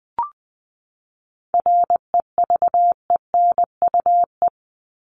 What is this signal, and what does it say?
Morse code, heard as a keyed tone, about 20 wpm, 700 Hz. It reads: REVENUE
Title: Top 1501-1750 Words 20wpm